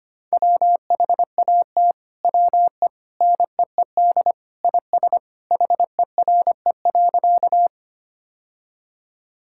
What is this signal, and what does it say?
Morse code, heard as a keyed tone, about 25 wpm, 700 Hz. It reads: WHAT WE NEED IS HERE.
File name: practice / 25wpm / Sentences with 2-Word Phrases - 25wpm